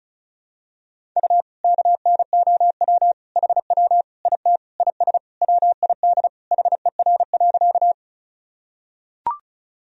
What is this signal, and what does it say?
Morse code, heard as a keyed tone, about 36 wpm, 700 Hz. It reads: U KNOW HW IT IS WID HER.
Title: Sentences from Top 100 Words and Abbreviations 35wpm